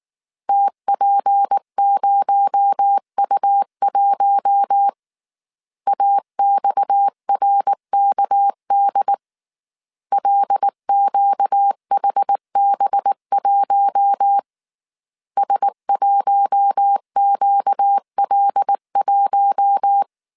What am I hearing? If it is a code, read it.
Morse code, heard as a keyed tone, about 19 wpm, 800 Hz. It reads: TP0U1 AXRKD LQHB1 S1QL1